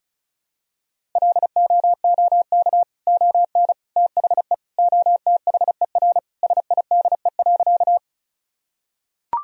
Morse code, heard as a keyed tone, about 36 wpm, 700 Hz. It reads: LOOK ON THE OTHER SIDE.
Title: Sentences from Top 1000 Words - Review Part 1 - 35wpm